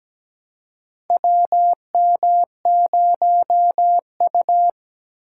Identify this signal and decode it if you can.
Morse code, heard as a keyed tone, about 17 wpm, 700 Hz. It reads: WM0U
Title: USA Call Signs - 2x Character Spacing - 17wpm